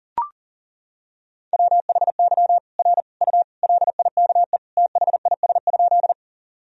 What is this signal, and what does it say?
Morse code, heard as a keyed tone, about 40 wpm, 700 Hz. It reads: WHY R U LIKE THIS?
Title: Sentences from Top 200 Words and Abbreviations 40wpm